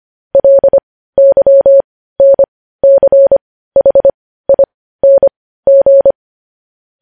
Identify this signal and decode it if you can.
Morse code, heard as a keyed tone, about 25 wpm, spaced as for 15 wpm, 550 Hz. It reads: LYNCHING
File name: podcast / News headlines-1x-25x15